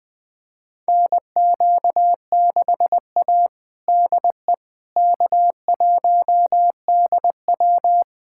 Morse code, heard as a keyed tone, about 21 wpm, 700 Hz. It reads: NQ6A DE K1DW
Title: USA Callsign DE Callsign - 4 Characters - 20wpm